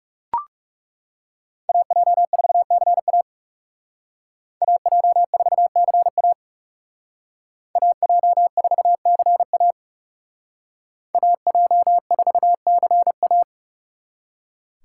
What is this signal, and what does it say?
Morse code, heard as a keyed tone, about 34 wpm, 700 Hz. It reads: AJ4CA AJ4CA AJ4CA AJ4CA
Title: Speed Racing - USA 2x2 Call Signs 30wpm